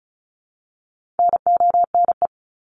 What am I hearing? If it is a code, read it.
Morse code, heard as a keyed tone, about 34 wpm, 700 Hz. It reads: NONE
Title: Top 1501-1750 Words 35wpm